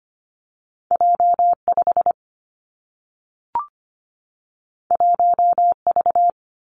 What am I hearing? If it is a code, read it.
Morse code, heard as a keyed tone, about 25 wpm, 700 Hz. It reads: J5 1V
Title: Two Challenging Letter-Number Combinations 25wpm